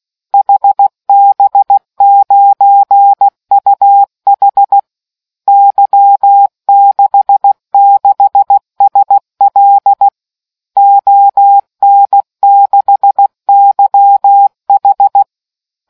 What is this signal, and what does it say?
Morse code, heard as a keyed tone, about 16 wpm, 800 Hz. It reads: HB9UH Y66SL ON6YH